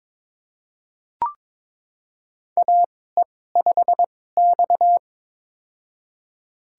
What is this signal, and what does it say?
Morse code, heard as a keyed tone, about 22 wpm, spaced as for 16 wpm, 700 Hz. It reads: AE5X
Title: USA Call Signs - 4 Characters - 2x Character Spacing - 22wpm